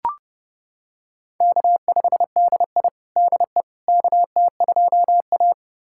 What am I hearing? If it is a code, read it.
Morse code, heard as a keyed tone, about 32 wpm, 700 Hz. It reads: K5DI DE KT2A
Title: USA Callsign DE Callsign - 4 Characters - 30wpm